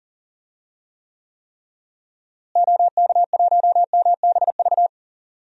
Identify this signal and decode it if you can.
Morse code, heard as a keyed tone, about 40 wpm, 700 Hz. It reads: OK1MBV
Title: DXCC-100 Call Signs - 6 Characters - Repeated 3x - 40wpm